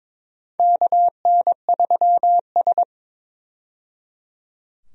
Morse code, heard as a keyed tone, about 22 wpm, 700 Hz. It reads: KN3S
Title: USA Call Signs - 4 Characters - 22wpm